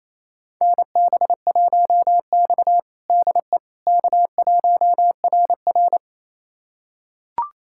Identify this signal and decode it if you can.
Morse code, heard as a keyed tone, about 28 wpm, 700 Hz. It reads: NB1X DE K1RR
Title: USA Callsign DE Callsign - 4 Characters - 28wpm